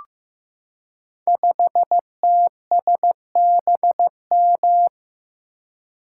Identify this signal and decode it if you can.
Morse code, heard as a keyed tone, about 15 wpm, 700 Hz. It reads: 5TSBM
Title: Code Groups - 5 Letter:Numbers - 15wpm